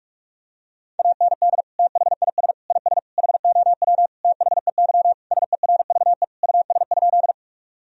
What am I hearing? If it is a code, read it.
Morse code, heard as a keyed tone, about 41 wpm, 700 Hz. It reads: AND THIS IS HOW THEY SERVE US?
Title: Sentences from Top 1000 Words - Review Part 2 - 45wpm